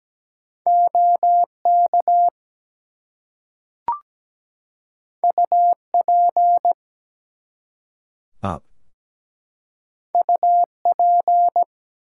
Morse code, heard as a keyed tone, about 17 wpm, 700 Hz. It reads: OK UP UP
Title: Two Letter Words - 17wpm